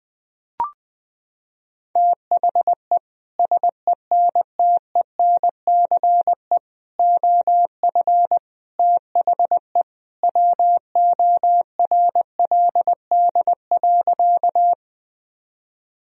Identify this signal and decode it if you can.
Morse code, heard as a keyed tone, about 20 wpm, 700 Hz. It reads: THE SENTENCE OF THE WORLD.
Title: Sentences from Top 1000 Words - Review Part 1 - 20wpm